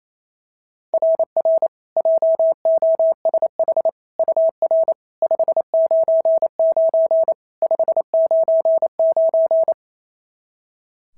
Morse code, heard as a keyed tone, about 25 wpm, 650 Hz. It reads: RR JOSH UR 599 599
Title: SOTA QSO Practice 28wpm